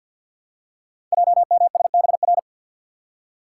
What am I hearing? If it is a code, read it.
Morse code, heard as a keyed tone, about 46 wpm, 700 Hz. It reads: JMSBR
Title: Code Groups - 5 Letters - 50wpm